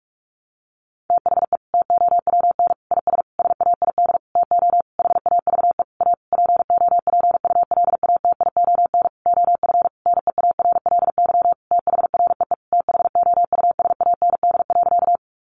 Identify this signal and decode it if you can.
Morse code, heard as a keyed tone, about 44 wpm, 700 Hz. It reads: THE TOWN IS SAID TO HAVE A POPULATION OF NEARLY THREE THOUSAND.
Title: Sentences from Top 1300 Words 45wpm